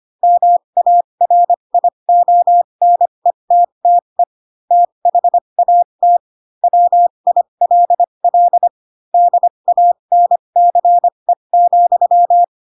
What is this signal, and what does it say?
Morse code, heard as a keyed tone, about 25 wpm, spaced as for 22 wpm, 700 Hz. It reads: MARIONETTE THAT WILL DANCE,
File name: Pinocchio / Pinocchio ch2 18 25 ik1hge